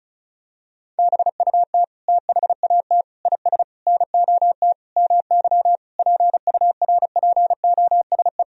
Morse code, heard as a keyed tone, about 36 wpm, 700 Hz. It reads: BUT THAT IS NOT MY PURPOSE
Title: Sentences from Top 1000 Words - Review Part 4 - 35wpm